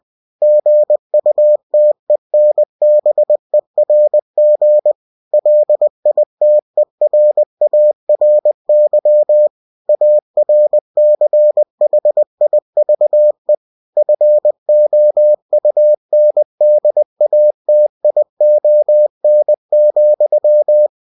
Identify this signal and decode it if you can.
Morse code, heard as a keyed tone, about 20 wpm, 600 Hz. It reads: GUTENBERG LITERARY ARCHIVE FOUNDATION,